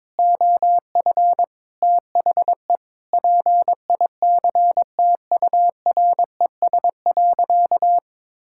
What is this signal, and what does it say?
Morse code, heard as a keyed tone, about 22 wpm, 700 Hz. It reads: OF THE PICTURES.
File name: Sentences with 7-Word Phrases - 22wpm